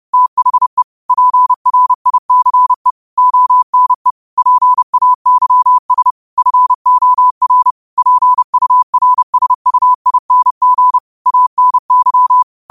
Morse code, heard as a keyed tone, about 30 wpm, 1000 Hz. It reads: THE PRICE ONE PAYS FOR PURSUING ANY
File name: qotdfiles / 30wpm